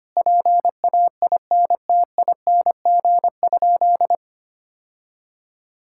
Morse code, heard as a keyed tone, about 25 wpm, 700 Hz. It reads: PAINTING?
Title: Sentences from Top 1000 Words - Review Part 6 - 25wpm